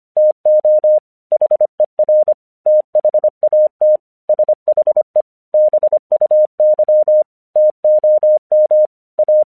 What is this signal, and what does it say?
Morse code, heard as a keyed tone, about 25 wpm, 610 Hz. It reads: TO HER THAT SHE BUY TOM A
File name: 1 English 25wpm / AQC17